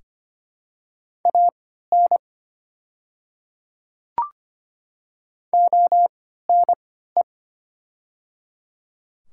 Morse code, heard as a keyed tone, about 25 wpm, spaced as for 14 wpm, 700 Hz. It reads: AN ONE
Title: 3x Character Spacing - Top 100 Words - 25wpm